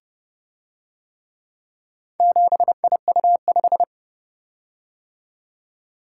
Morse code, heard as a keyed tone, about 30 wpm, 700 Hz. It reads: 7IU5